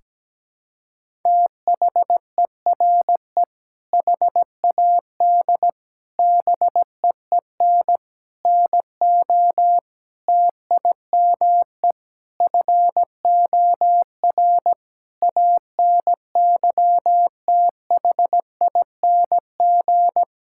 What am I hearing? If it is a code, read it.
Morse code, heard as a keyed tone, about 17 wpm, 700 Hz. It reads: THERE HAD BEEN NO TIME FOR ANYTHING
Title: Sentences from Top 1600 Words 17wpm